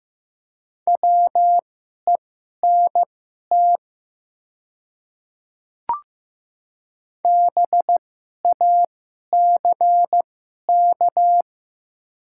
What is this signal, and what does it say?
Morse code, heard as a keyed tone, about 15 wpm, spaced as for 11 wpm, 700 Hz. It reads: WENT BACK
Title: 2x Character Spacing - Top 200 Words - 15wpm